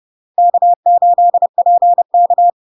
Morse code, heard as a keyed tone, about 30 wpm, 700 Hz. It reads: K8PK
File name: LARK-FD-2022-Log-in-CW-30wpm-700-Hz